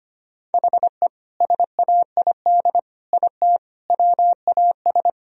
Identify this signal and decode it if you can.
Morse code, heard as a keyed tone, about 25 wpm, 700 Hz. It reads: HE SAID IT WAS
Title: Sentences from Top 1000 Words - Review Part 2 - 25wpm